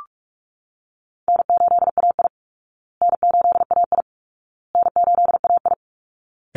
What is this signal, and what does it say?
Morse code, heard as a keyed tone, about 44 wpm, 700 Hz. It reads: N8AI N8AI N8AI
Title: Easy Call Signs Repeated 3x - 45wpm